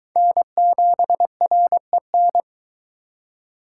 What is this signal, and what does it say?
Morse code, heard as a keyed tone, about 22 wpm, 700 Hz. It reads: N7REN